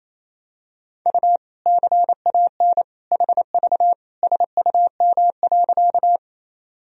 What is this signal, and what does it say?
Morse code, heard as a keyed tone, about 28 wpm, 700 Hz. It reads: U CAN HV SUM.
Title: Sentences from Top 100 Words and Abbreviations 28wpm